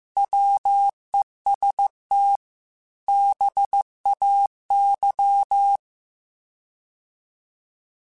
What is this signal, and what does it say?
Morse code, heard as a keyed tone, about 15 wpm, 800 Hz. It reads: WEST BAY